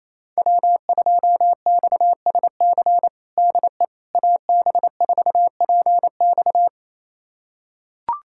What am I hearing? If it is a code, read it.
Morse code, heard as a keyed tone, about 28 wpm, 700 Hz. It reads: W2XSC DE AB4PX
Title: USA Callsign DE Callsign - 5 Characters - 28wpm